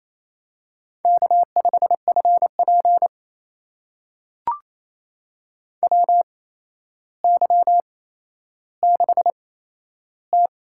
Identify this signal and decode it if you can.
Morse code, heard as a keyed tone, about 27 wpm, spaced as for 12 wpm, 700 Hz. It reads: K5FP WY6T
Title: USA Call Signs - 4 Characters - 8x Character Spacing - 28wpm